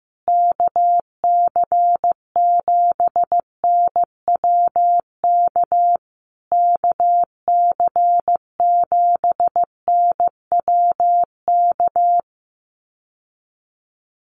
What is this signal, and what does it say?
Morse code, heard as a keyed tone, about 15 wpm, 700 Hz. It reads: KC7NWK KC7NWK
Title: USA Call Signs Repeated 3x 15wpm